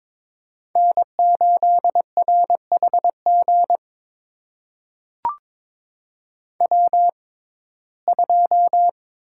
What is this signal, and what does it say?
Morse code, heard as a keyed tone, about 22 wpm, spaced as for 16 wpm, 700 Hz. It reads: N8RHG W2
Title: USA Call Signs - 5 Characters - 6x Character Spacing - 22wpm